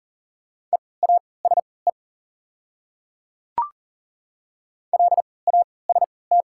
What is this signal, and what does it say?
Morse code, heard as a keyed tone, about 38 wpm, spaced as for 22 wpm, 700 Hz. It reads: EASE LAST
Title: 3x Character Spacing - Top 300 Words - 40wpm